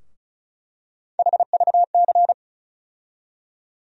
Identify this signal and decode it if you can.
Morse code, heard as a keyed tone, about 36 wpm, 700 Hz. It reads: HVC
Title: USA Call Sign Postfixes - 3 Letters 35wpm